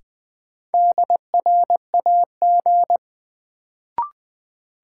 Morse code, heard as a keyed tone, about 20 wpm, 700 Hz. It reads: DRAG
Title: Sentences from Top 1500 Words - New Word Drill - 20wpm